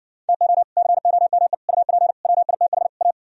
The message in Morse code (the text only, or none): TO COME UP WITH A